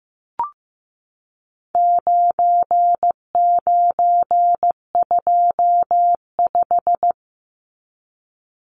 Morse code, heard as a keyed tone, about 15 wpm, 700 Hz. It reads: 9925
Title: Numbers - 4-Digit 15wpm